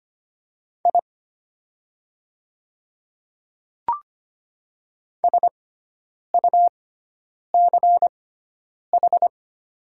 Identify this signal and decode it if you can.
Morse code, heard as a keyed tone, about 25 wpm, spaced as for 9 wpm, 690 Hz. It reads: I SUCH
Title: 6x Character Spacing - Top 200 Words - 25wpm